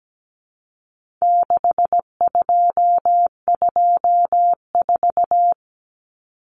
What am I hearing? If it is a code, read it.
Morse code, heard as a keyed tone, about 17 wpm, 700 Hz. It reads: 6224
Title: Common Cut Numbers - 4-Digit 17wpm